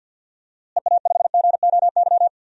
Morse code, heard as a keyed tone, about 50 wpm, 700 Hz. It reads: EA5GOY
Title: DXCC-100 Call Signs - 6 Characters - Repeated 3x - 50wpm